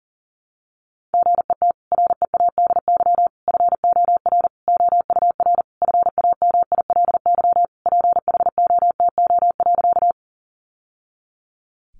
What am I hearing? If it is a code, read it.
Morse code, heard as a keyed tone, about 40 wpm, 700 Hz. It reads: GET READY FOR OUR FAMILY PHOTO.
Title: Sentences from Top 1200 Words 40wpm